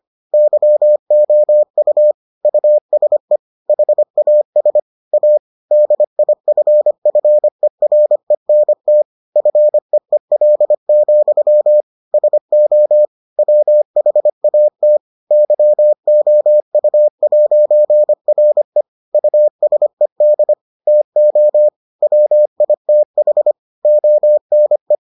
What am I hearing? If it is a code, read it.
Morse code, heard as a keyed tone, about 25 wpm, 600 Hz. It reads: YOU USE HAS A DIFFERENT FEEL, SO WHAT YOU'RE USED TO WITH ONE